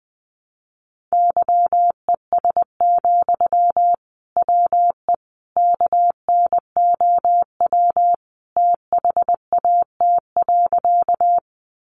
Morse code, heard as a keyed tone, about 20 wpm, 700 Hz. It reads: YES, WE KNOW THAT.
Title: Sentences from Top 500 Words 20wpm